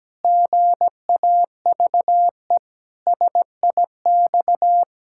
Morse code, heard as a keyed tone, about 17 wpm, 700 Hz. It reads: GAVE SIX
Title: Pangram Challenge v2 - 17wpm